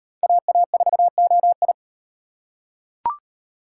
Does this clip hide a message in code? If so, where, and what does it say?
Yes: Morse code from 0.2 to 1.7 s, heard as a keyed tone, about 38 wpm, 700 Hz. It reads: AA4OI